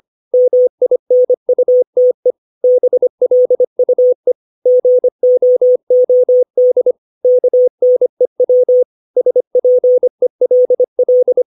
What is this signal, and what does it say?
Morse code, heard as a keyed tone, about 25 wpm, 500 Hz. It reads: MINUTE BLUE GOOD KNEW SPELL
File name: 632 words / 632 Words 25WPM